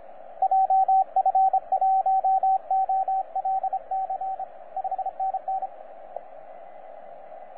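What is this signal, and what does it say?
Morse code, heard as a keyed tone, about 28 wpm, 770 Hz, with radio noise. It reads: JF1OLC 5NN
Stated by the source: 3D2SE heard on 15 metres CW, a DX World.net recording.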